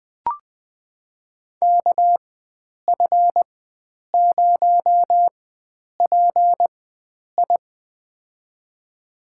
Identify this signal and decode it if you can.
Morse code, heard as a keyed tone, about 20 wpm, spaced as for 9 wpm, 700 Hz. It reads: KF0PI
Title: USA Call Signs - 5 Characters - 4x Character Spacing - 20wpm